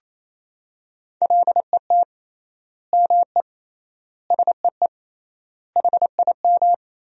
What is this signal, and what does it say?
Morse code, heard as a keyed tone, about 28 wpm, 700 Hz. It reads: LET ME SEE HIM
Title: 3x Word-Spacing - Sentences from Top 300 Words - 28wpm